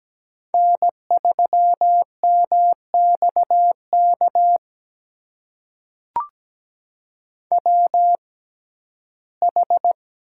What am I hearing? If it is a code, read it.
Morse code, heard as a keyed tone, about 17 wpm, spaced as for 12 wpm, 700 Hz. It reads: N3MXK WH
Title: USA Call Signs - 5 Characters - 6x Character Spacing - 17wpm